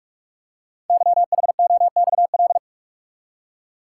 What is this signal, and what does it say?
Morse code, heard as a keyed tone, about 44 wpm, 700 Hz. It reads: YHOXL
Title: Code Groups - 5 Letters - 45wpm